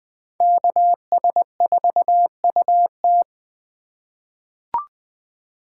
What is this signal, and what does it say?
Morse code, heard as a keyed tone, about 20 wpm, 700 Hz. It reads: KS4UT
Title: USA Call Signs - 2x Character Spacing - 20wpm